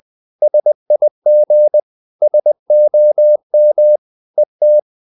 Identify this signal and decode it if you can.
Morse code, heard as a keyed tone, about 20 wpm, 600 Hz. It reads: SIG SOM ET
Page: Ferie 19 wpm